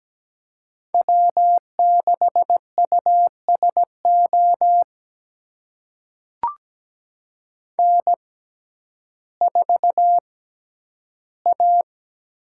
Morse code, heard as a keyed tone, about 17 wpm, spaced as for 10 wpm, 700 Hz. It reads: W6USO N4A
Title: USA Call Signs - 6x Character Spacing - 17wpm